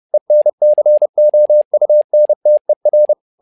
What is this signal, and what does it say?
Morse code, heard as a keyed tone, about 30 wpm, 600 Hz. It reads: ENCOUNTER
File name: learning center / tower trouble / tower trouble 30wpm